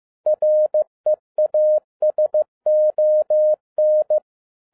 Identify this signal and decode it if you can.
Morse code, heard as a keyed tone, about 15 wpm, 610 Hz. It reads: REASON